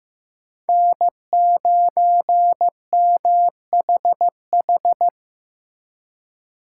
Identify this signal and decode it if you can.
Morse code, heard as a keyed tone, about 18 wpm, 700 Hz. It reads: N9MHH
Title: Speed Racing - USA 1x3 Call Signs 15wpm